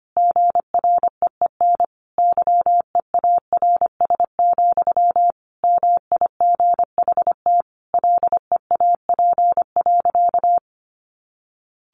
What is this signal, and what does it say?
Morse code, heard as a keyed tone, about 25 wpm, 700 Hz. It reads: GREEN YEARS, MIGHT LEAP.
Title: A Sound Of Thunder 25wpm